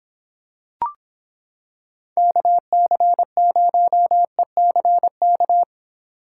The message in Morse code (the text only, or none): KC0ECK